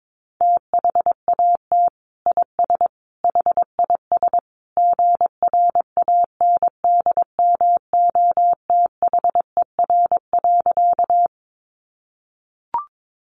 Morse code, heard as a keyed tone, about 22 wpm, 700 Hz. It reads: THAT IS HIS GRANDMOTHER.
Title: Sentences from Top 1200 Words 22wpm